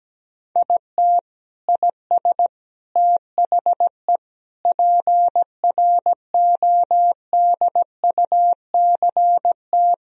Morse code, heard as a keyed tone, about 17 wpm, 700 Hz. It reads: IT IS THE PRODUCT
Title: Sentences from Top 1000 Words - Review Part 2 - 17wpm